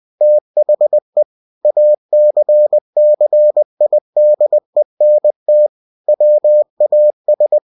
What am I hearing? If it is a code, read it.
Morse code, heard as a keyed tone, about 20 wpm, 600 Hz. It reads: THE ACCIDENT WAS
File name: winnie-20wpm